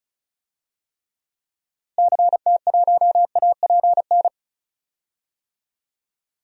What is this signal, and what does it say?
Morse code, heard as a keyed tone, about 34 wpm, 700 Hz. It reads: CT1APN
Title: DXCC-100 Call Signs - 6 Characters - Repeated 2x - 35wpm